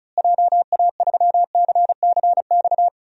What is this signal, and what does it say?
Morse code, heard as a keyed tone, about 34 wpm, 700 Hz. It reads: JA3CCX
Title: DXCC-100 Call Signs - 6 Characters - Repeated 3x - 35wpm